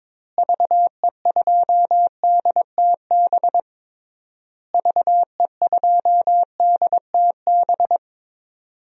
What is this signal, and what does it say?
Morse code, heard as a keyed tone, about 22 wpm, 700 Hz. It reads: VE2DTB VE2DTB
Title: DXCC-100 Call Signs - 6 Characters - Repeated 3x - 22wpm